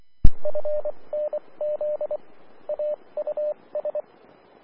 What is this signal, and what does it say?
Morse code, heard as a keyed tone, about 24 wpm, spaced as for 20 wpm, 600 Hz, with radio noise. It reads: FNZ AUS
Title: Austin, TX Amateur Repeaters